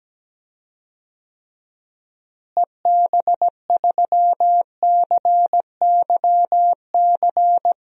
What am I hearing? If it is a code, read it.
Morse code, heard as a keyed tone, about 17 wpm, 700 Hz. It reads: EB3CYC
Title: DXCC-100 Call Signs - 6 Characters - Repeated 3x - 17wpm